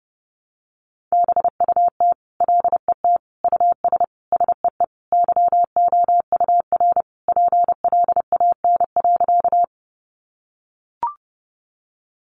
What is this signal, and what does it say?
Morse code, heard as a keyed tone, about 30 wpm, 700 Hz. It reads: BUT LET US SEE YOUR PLAN.
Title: Sentences from Top 500 Words 30wpm